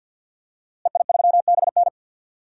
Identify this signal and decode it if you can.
Morse code, heard as a keyed tone, about 47 wpm, 700 Hz. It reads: EI3BN
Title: DXCC-100 Call Signs - 5 Characters - Repeated 3x - 50wpm